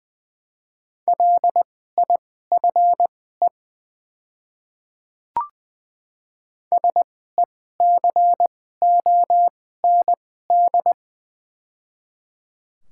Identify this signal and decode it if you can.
Morse code, heard as a keyed tone, about 21 wpm, spaced as for 15 wpm, 700 Hz. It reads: LIFE SECOND
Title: 2x Character Spacing - Top 300 Words - 20wpm